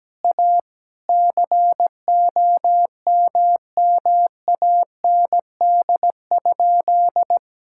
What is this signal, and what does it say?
Morse code, heard as a keyed tone, about 17 wpm, 700 Hz. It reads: A COMMAND?
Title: Sentences from Top 1600 Words 17wpm